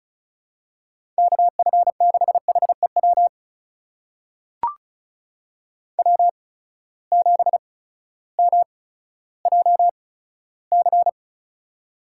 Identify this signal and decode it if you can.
Morse code, heard as a keyed tone, about 36 wpm, spaced as for 16 wpm, 700 Hz. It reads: KF6HEW W7MJC
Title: USA Call Signs - 8x Character Spacing - 35wpm